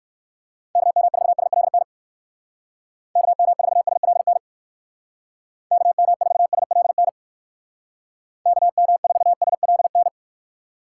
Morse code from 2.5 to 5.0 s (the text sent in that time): KM4SLN